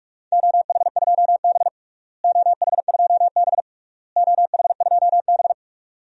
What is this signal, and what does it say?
Morse code, heard as a keyed tone, about 46 wpm, 700 Hz. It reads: OH2B OH2B OH2B
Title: DXCC-100 Call Signs - 4 Characters - Repeated 3x - 45wpm